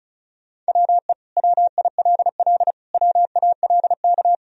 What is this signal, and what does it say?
Morse code, heard as a keyed tone, about 34 wpm, 700 Hz. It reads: WE WILL WALK
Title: Sentences from Top 1000 Words - Review Part 1 - 35wpm